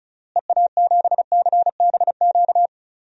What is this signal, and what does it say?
Morse code, heard as a keyed tone, about 34 wpm, 700 Hz. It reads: EA7CBQ
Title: DXCC-100 Call Signs - 6 Characters - Repeated 3x - 35wpm